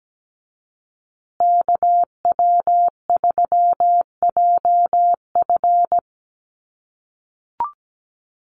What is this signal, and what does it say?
Morse code, heard as a keyed tone, about 17 wpm, 700 Hz. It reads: KW3JF
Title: USA Call Signs Repeated 3x 17wpm